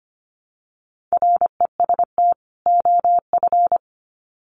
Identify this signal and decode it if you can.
Morse code, heard as a keyed tone, about 25 wpm, 700 Hz. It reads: REST OF
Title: Sets of 2 Words 25wpm